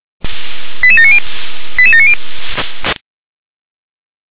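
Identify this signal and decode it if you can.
Five-tone selective call (ZVEI), heard as quick runs of five tones, tones of about 70 ms. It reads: 88790 88790
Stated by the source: ZVEI five-tone selective call 88635, received by SDR.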